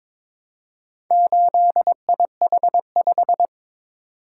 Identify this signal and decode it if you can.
Morse code, heard as a keyed tone, about 22 wpm, 700 Hz. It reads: 8IH5
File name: practice / 22wpm / Four Letter-Number Combinations - 22wpm